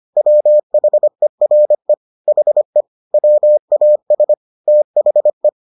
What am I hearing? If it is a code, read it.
Morse code, heard as a keyed tone, about 25 wpm, 600 Hz. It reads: WHERE HE WAS THE